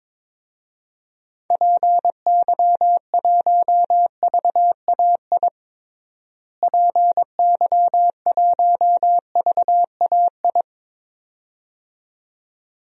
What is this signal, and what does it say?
Morse code, heard as a keyed tone, about 22 wpm, 700 Hz. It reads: PY1VAI PY1VAI
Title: DXCC-100 Call Signs - 6 Characters - Repeated 2x - 22wpm